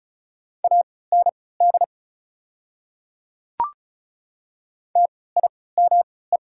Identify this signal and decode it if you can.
Morse code, heard as a keyed tone, about 34 wpm, spaced as for 20 wpm, 690 Hz. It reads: AND TIME
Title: 3x Character Spacing - Top 200 Words - 35wpm